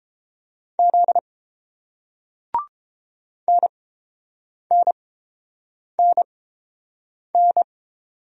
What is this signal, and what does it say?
Morse code, heard as a keyed tone, about 30 wpm, spaced as for 7 wpm, 690 Hz. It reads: Z NNNN